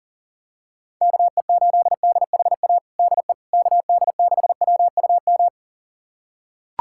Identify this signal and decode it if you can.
Morse code, heard as a keyed tone, about 40 wpm, 700 Hz. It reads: KE8DHA DE KD6WUM
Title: USA Callsign DE Callsign - 6 Characters - 40wpm